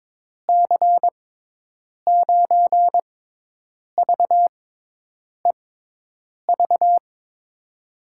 Morse code, heard as a keyed tone, about 22 wpm, spaced as for 8 wpm, 700 Hz. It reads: C9VEV
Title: USA Call Signs - 6 Characters - 6x Character Spacing - 22wpm